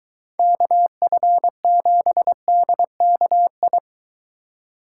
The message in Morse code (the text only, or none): KF7DKI